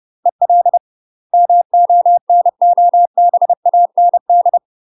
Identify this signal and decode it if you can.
Morse code, heard as a keyed tone, about 30 wpm, 700 Hz. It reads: EL MONOBAND
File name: UK MASTER QSOs TRAINING / 30WPM / UK MASTER QSOs 0002